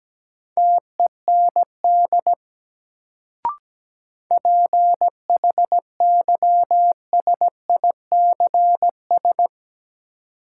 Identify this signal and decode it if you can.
Morse code, heard as a keyed tone, about 17 wpm, 700 Hz. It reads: TEND PHYSICS
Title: Sentences from Top 1000 Words - Review Part 5 - New Word Drill - 17wpm